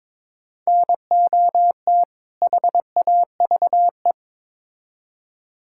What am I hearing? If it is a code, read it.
Morse code, heard as a keyed tone, about 22 wpm, 700 Hz. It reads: NOT HAVE
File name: Build A Phrase - 2-Word Phrases - 22wpm